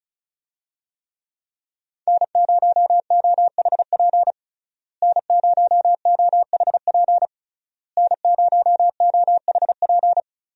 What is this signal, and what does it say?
Morse code, heard as a keyed tone, about 36 wpm, 700 Hz. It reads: N0OHP N0OHP N0OHP
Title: USA Call Signs - 5 Characters - Repeated 3x - 35wpm